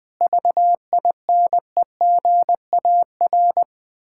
Morse code, heard as a keyed tone, about 20 wpm, 690 Hz. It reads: VINEGAR
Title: Difficult Words with Hints - 20wpm